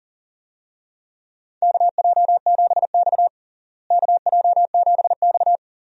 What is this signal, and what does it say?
Morse code, heard as a keyed tone, about 40 wpm, 700 Hz. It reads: KJ7X KJ7X
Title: USA Call Signs - 4 Characters - Repeated 3x - 40wpm